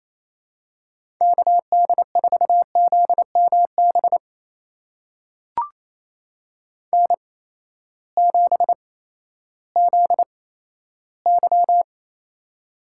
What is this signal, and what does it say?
Morse code, heard as a keyed tone, about 28 wpm, spaced as for 14 wpm, 700 Hz. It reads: KD4ZMB N7ZY
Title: USA Call Signs - 8x Character Spacing - 28wpm